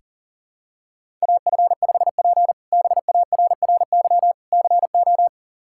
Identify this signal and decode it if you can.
Morse code, heard as a keyed tone, about 40 wpm, 700 Hz. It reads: AF5P BARRY CO
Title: K1USN SST QSO Practice - 40wpm